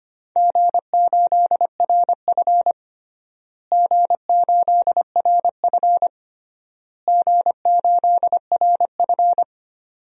Morse code, heard as a keyed tone, about 25 wpm, 700 Hz. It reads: G8RF G8RF G8RF
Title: DXCC-100 Call Signs - 4 Characters - Repeated 3x - 25wpm